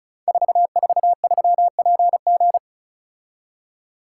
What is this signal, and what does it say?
Morse code, heard as a keyed tone, about 36 wpm, 690 Hz. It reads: 443PG
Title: Code Groups - 5 Letter:Numbers - 35wpm